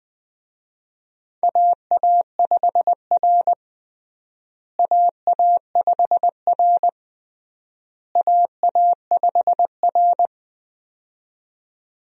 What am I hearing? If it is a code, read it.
Morse code, heard as a keyed tone, about 20 wpm, 700 Hz. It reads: AA5R AA5R AA5R
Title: USA Call Signs - 4 Characters - Repeated 3x - 20wpm